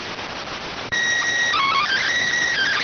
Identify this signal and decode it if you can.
SSTV (slow-scan television), PD 290.